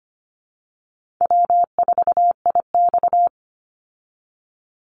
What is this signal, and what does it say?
Morse code, heard as a keyed tone, about 25 wpm, 700 Hz. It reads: W4IX
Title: Call Signs Repeated 2x - 25wpm